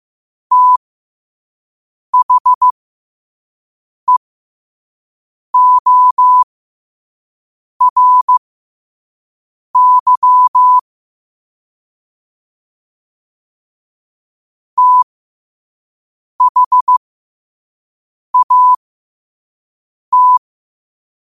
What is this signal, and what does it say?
Morse code, heard as a keyed tone, about 15 wpm, spaced as for 5 wpm, 1000 Hz. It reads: THEORY THAT
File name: qotdfiles / 5wpm